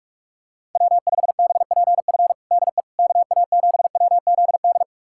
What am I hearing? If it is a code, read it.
Morse code, heard as a keyed tone, about 42 wpm, 700 Hz. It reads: W5BPF DE KA7WZD
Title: USA Callsign DE Callsign 45wpm